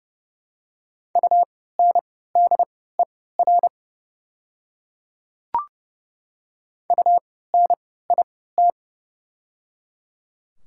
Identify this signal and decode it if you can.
Morse code, heard as a keyed tone, about 30 wpm, spaced as for 17 wpm, 690 Hz. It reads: UNDER UNIT
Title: 3x Character Spacing - Top 500 Words - 30wpm